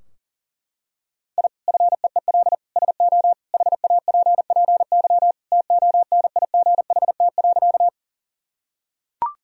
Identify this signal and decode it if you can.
Morse code, heard as a keyed tone, about 37 wpm, 700 Hz. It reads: I FEEL SO HAPPY TONIGHT.
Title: Sentences from Top 1000 Words - Review Part 3 - 40wpm